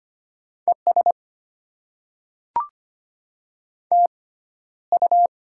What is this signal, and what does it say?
Morse code, heard as a keyed tone, about 25 wpm, spaced as for 13 wpm, 700 Hz. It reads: ES TU
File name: Top 20 QSO Elements - 6x Character Spacing - 25wpm